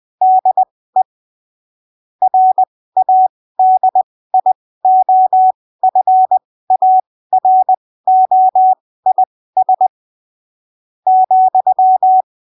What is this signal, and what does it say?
Morse code, heard as a keyed tone, about 20 wpm, spaced as for 15 wpm, 750 Hz. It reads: DE RADIOFAROIS ,
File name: arquivos / texto / texto 03 20ppm